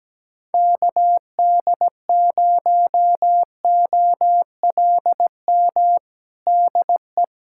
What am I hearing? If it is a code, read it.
Morse code, heard as a keyed tone, about 17 wpm, 700 Hz. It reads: KD0OLM DE
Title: USA Callsign DE Callsign - 6 Characters - 17wpm